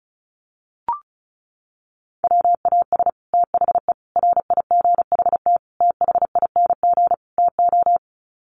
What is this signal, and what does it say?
Morse code, heard as a keyed tone, about 34 wpm, 700 Hz. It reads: WAS THE RIGHT THING TO
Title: Sets of 5 Words Encore2 35wpm